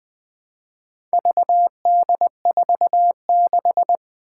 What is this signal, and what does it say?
Morse code, heard as a keyed tone, about 20 wpm, 700 Hz. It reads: VD46
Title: Four Challenging Letter-Number Combinations 20wpm